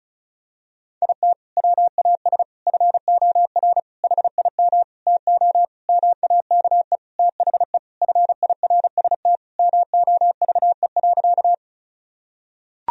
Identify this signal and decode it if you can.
Morse code, heard as a keyed tone, about 36 wpm, 700 Hz. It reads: IT WAS FOR HIM TO MAKE THE FIRST MOVE.
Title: Sentences from Top 1000 Words - Review Part 1 - 35wpm